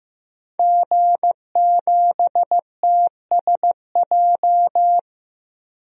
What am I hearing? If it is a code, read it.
Morse code, heard as a keyed tone, about 15 wpm, 700 Hz. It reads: G7TSJ
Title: DXCC-100 Call Signs - 5 Characters - 15wpm